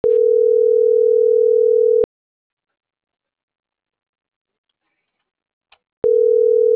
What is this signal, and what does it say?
Ringback tone (North America), 440 + 480 Hz.